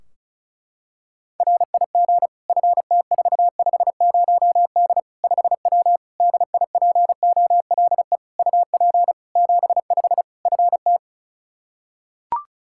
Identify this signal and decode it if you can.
Morse code, heard as a keyed tone, about 36 wpm, 700 Hz. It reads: RIG FT450D 5W DIPOLE UP 75 FT
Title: Practice QSO - Exchange 2 - Info - 35wpm